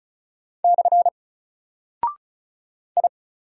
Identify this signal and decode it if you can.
Morse code, heard as a keyed tone, about 36 wpm, 700 Hz. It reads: / I
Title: Single Character (Letter:Number:Punctuation:Prosign) Alt 35wpm